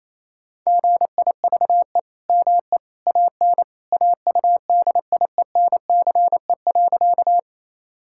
Morse code, heard as a keyed tone, about 28 wpm, 700 Hz. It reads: GIVE ME AN AUDIENCE.
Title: Sentences from Top 1000 Words - Review Part 5 - 28wpm